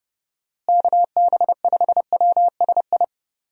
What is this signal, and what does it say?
Morse code, heard as a keyed tone, about 32 wpm, 700 Hz. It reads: KB5WSI
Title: USA Call Signs - 6 Characters - Repeated 3x - 30wpm